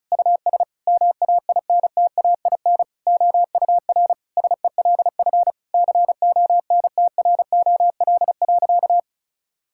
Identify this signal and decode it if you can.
Morse code, heard as a keyed tone, about 36 wpm, 700 Hz. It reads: US MAINTAIN OUR SELF CONTROL.
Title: Sentences from Top 1000 Words - Review Part 5 - 35wpm